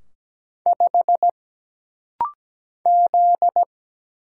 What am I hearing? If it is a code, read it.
Morse code, heard as a keyed tone, about 17 wpm, 700 Hz. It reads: 5 Z
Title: Single Letter-Number - Rapid-Fire 17wpm